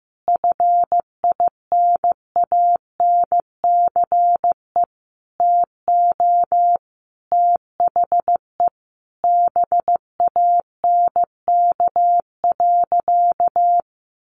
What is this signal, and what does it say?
Morse code, heard as a keyed tone, about 15 wpm, 700 Hz. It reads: FINANCE TO THE BANK.
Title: Sentences from Top 1500 Words 15wpm